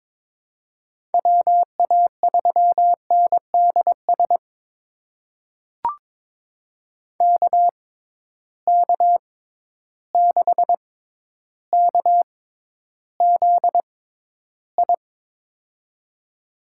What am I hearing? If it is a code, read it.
Morse code, heard as a keyed tone, about 22 wpm, spaced as for 11 wpm, 700 Hz. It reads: WA3NDS KK6KZI
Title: USA Call Signs - 6x Character Spacing - 22wpm